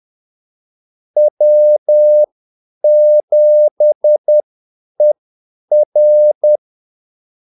Morse code, heard as a keyed tone, about 10 wpm, spaced as for 8 wpm, 600 Hz. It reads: W7ER
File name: Callsigns-10-8wpm